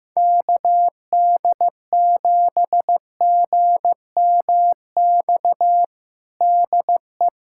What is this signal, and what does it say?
Morse code, heard as a keyed tone, about 15 wpm, 700 Hz. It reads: KD7GMX DE
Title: USA Callsign DE Callsign - 6 Characters - 15wpm